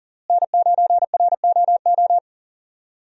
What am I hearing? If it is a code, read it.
Morse code, heard as a keyed tone, about 40 wpm, 700 Hz. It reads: N9ROO
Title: USA Call Signs - 5 Characters - Repeated 3x - 40wpm